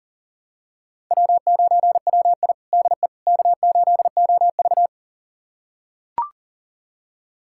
Morse code, heard as a keyed tone, about 40 wpm, 700 Hz. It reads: W9WI DE K8OV
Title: USA Callsign DE Callsign - 4 Characters - 40wpm